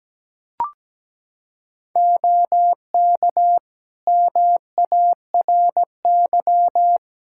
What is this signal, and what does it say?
Morse code, heard as a keyed tone, about 17 wpm, 700 Hz. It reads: OK MARY
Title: Practice QSO - Exchange 3 - More Info - 17wpm